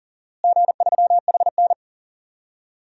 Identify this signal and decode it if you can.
Morse code, heard as a keyed tone, about 40 wpm, 700 Hz. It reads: G3HN
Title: DXCC-100 Call Signs - 4 Characters - Repeated 3x - 40wpm